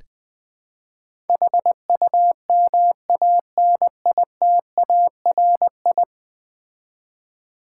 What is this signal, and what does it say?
Morse code, heard as a keyed tone, about 20 wpm, 700 Hz. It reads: HUMANITARI
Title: Build A Word - 5 Letters At A Time - 12-Letter Words - 20wpm